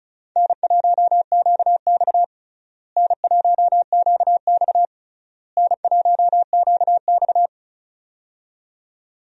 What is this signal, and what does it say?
Morse code, heard as a keyed tone, about 36 wpm, 700 Hz. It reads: N1QX N1QX N1QX
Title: USA Call Signs - 4 Characters - Repeated 3x - 35wpm